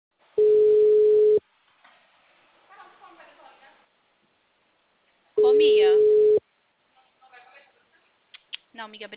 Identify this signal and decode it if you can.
Ringback tone (Europe, 425 Hz), with speech.